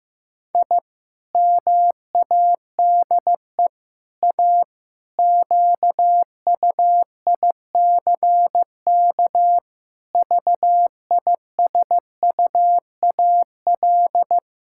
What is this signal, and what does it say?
Morse code, heard as a keyed tone, about 16 wpm, 700 Hz. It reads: I MADE A QUICK VISUAL
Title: Sentences from Top 1000 Words - Review Part 6 - 15wpm